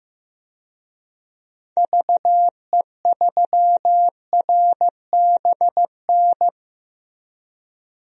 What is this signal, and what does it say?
Morse code, heard as a keyed tone, about 15 wpm, 700 Hz. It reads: VE3RBN
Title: DXCC-100 Call Signs - 6 Characters - 15wpm